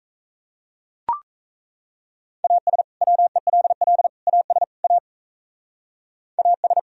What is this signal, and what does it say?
Morse code, heard as a keyed tone, about 35 wpm, 700 Hz. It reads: AS WELL AS A AS
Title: Speed-Racing - Sets of 4 Words 28wpm